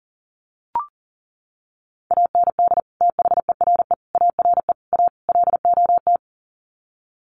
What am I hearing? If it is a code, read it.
Morse code, heard as a keyed tone, about 40 wpm, 700 Hz. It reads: AND THERE ARE A LOT
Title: Sets of 5 Words Encore2 40wpm